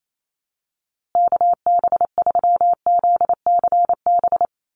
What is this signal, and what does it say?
Morse code, heard as a keyed tone, about 28 wpm, 700 Hz. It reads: KB3ZCB
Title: USA Call Signs Repeated 2x 28wpm